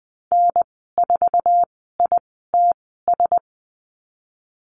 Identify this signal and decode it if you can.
Morse code, heard as a keyed tone, about 20 wpm, spaced as for 15 wpm, 700 Hz. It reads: N4ITS
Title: 048 Lesson 9g - Number 4 - Call Signs 20wpm